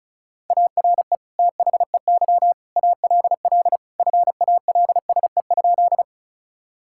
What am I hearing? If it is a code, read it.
Morse code, heard as a keyed tone, about 36 wpm, 700 Hz. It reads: ARE THEY ALL FALSE?
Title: Sentences from Top 1600 Words 35wpm